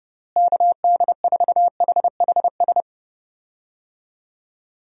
Morse code, heard as a keyed tone, about 30 wpm, 700 Hz. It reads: KD4HHS
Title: USA Call Signs - 6 Characters - Repeated 2x - 30wpm